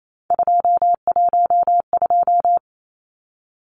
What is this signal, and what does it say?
Morse code, heard as a keyed tone, about 28 wpm, 700 Hz. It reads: 212
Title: Numbers - 3-Digit 28wpm